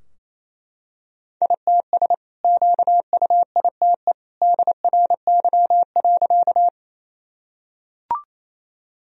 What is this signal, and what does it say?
Morse code, heard as a keyed tone, about 28 wpm, 700 Hz. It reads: ITS QUITE DRY.
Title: Sentences from Top 1000 Words - Review Part 3 - 28wpm